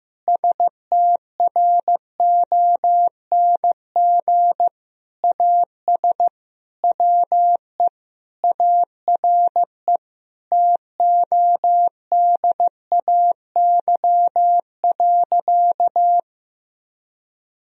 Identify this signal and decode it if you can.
Morse code, heard as a keyed tone, about 15 wpm, 700 Hz. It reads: STRONG AS WE ARE TODAY.
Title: Sentences from Top 1000 Words - Review Part 4 - 15wpm